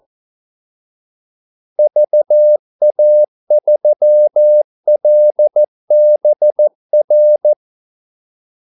Morse code, heard as a keyed tone, about 14 wpm, 600 Hz. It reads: VA3LBR
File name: Callsigns-14wpm